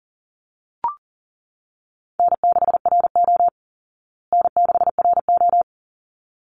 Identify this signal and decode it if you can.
Morse code, heard as a keyed tone, about 40 wpm, 700 Hz. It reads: N6RO N6RO
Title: Easy Call Signs Repeated 2x - 40wpm